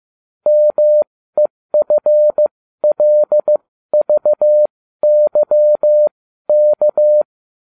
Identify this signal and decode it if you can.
Morse code, heard as a keyed tone, about 15 wpm, spaced as for 13 wpm, 600 Hz. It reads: MEFLVYK